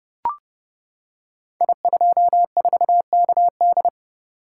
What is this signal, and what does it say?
Morse code, heard as a keyed tone, about 30 wpm, 700 Hz. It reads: I24KD
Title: Code Groups - 5 Letter:Numbers - 30wpm